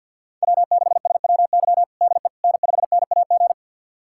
Morse code, heard as a keyed tone, about 50 wpm, 690 Hz. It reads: W6SWY DE N5NAG
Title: USA Callsign DE Callsign 50wpm